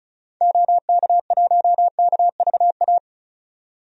Morse code, heard as a keyed tone, about 34 wpm, 700 Hz. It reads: OK1KVA
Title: DXCC-100 Call Signs - 6 Characters - 35wpm